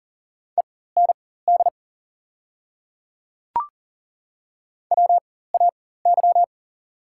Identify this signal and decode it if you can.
Morse code, heard as a keyed tone, about 40 wpm, spaced as for 19 wpm, 690 Hz. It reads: END WAY
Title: 4x Character Spacing - Top 200 Words - 40wpm